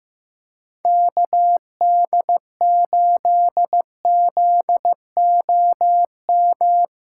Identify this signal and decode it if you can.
Morse code, heard as a keyed tone, about 15 wpm, 700 Hz. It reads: KD8ZOM
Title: USA Call Signs - 6 Characters - 2x Character Spacing - 15wpm